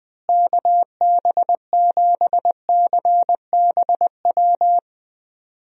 Morse code, heard as a keyed tone, about 20 wpm, 700 Hz. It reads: KB7CBW
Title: USA Call Signs - 6 Characters - Repeated 3x - 20wpm